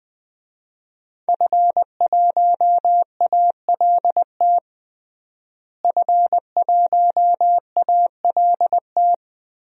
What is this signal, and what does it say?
Morse code, heard as a keyed tone, about 20 wpm, 700 Hz. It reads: F1ALT F1ALT
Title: DXCC-100 Call Signs - 5 Characters - Repeated 3x - 20wpm